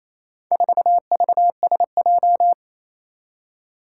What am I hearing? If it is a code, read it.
Morse code, heard as a keyed tone, about 27 wpm, 700 Hz. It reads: 4VSJ
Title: Four Challenging Letter-Number Combinations 28wpm